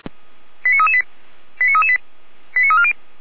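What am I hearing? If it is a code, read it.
Five-tone selective call (ZVEI), heard as quick runs of five tones, tones of about 70 ms. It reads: 89338 89338 89340
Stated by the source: ZVEI five-tone selective call received on SDR.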